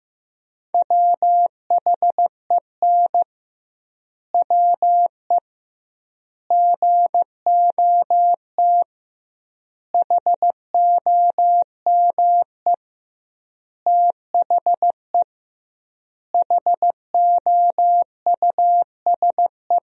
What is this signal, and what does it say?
Morse code, heard as a keyed tone, about 15 wpm, 700 Hz. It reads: WHEN WE GOT HOME THE HOUSE
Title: 2x Word-Spacing - Sentences from Top 500 Words - 15wpm